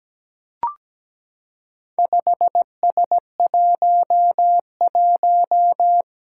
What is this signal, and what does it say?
Morse code, heard as a keyed tone, about 17 wpm, 700 Hz. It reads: 5S11
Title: Four Challenging Letter-Number Combinations 17wpm